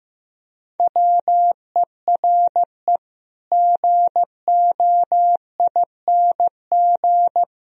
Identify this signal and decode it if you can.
Morse code, heard as a keyed tone, about 15 wpm, 700 Hz. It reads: WERE GOING
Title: Speed-Racing - Sets of 3 Words 15wpm